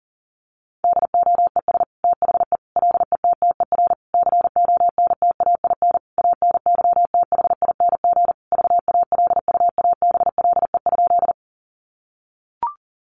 Morse code, heard as a keyed tone, about 40 wpm, 700 Hz. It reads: DOES THE LETTER CONTAIN ANYTHING VALUABLE?
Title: Sentences from Top 1100 Words 40wpm